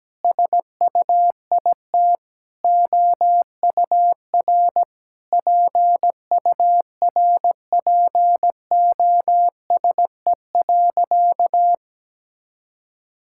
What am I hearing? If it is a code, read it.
Morse code, heard as a keyed tone, about 17 wpm, 700 Hz. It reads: SUIT OUR PURPOSE.
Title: Sentences from Top 1600 Words 17wpm